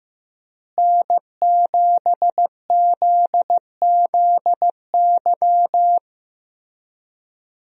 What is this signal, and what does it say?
Morse code, heard as a keyed tone, about 15 wpm, 700 Hz. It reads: N7ZZY
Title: USA Call Signs - 5 Characters - Repeated 3x - 15wpm